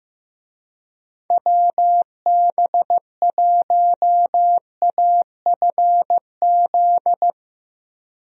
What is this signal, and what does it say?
Morse code, heard as a keyed tone, about 15 wpm, 700 Hz. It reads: WB1AFZ
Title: USA Call Signs - 6 Characters - Repeated 3x - 15wpm